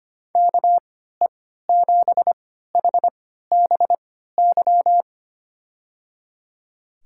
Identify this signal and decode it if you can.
Morse code, heard as a keyed tone, about 25 wpm, spaced as for 14 wpm, 700 Hz. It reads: KE7HBY
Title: USA Call Signs - 6 Characters - 3x Character Spacing - 25wpm